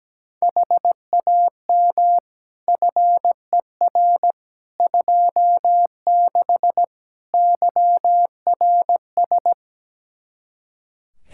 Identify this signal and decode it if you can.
Morse code, heard as a keyed tone, about 17 wpm, 700 Hz. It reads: HAM FER 26 YRS
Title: Practice QSO - Exchange 3 - More Info - 17wpm